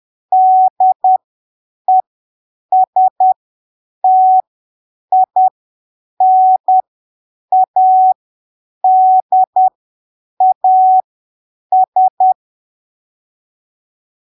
Morse code, heard as a keyed tone, about 10 wpm, spaced as for 7 wpm, 750 Hz. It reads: DESTINADAS
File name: arquivos / texto / texto 01 10ppm